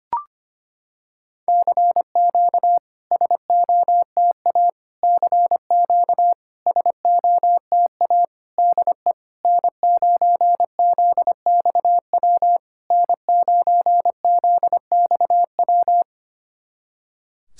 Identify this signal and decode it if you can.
Morse code, heard as a keyed tone, about 25 wpm, 700 Hz. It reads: CQ SOTA CQ SOTA DE N9ZXW N9ZXW
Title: SOTA QSO Practice 25wpm